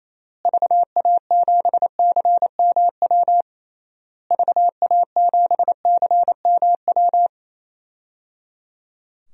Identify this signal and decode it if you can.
Morse code, heard as a keyed tone, about 28 wpm, 700 Hz. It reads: VA7CMW VA7CMW
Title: DXCC-100 Call Signs - 6 Characters - Repeated 3x - 28wpm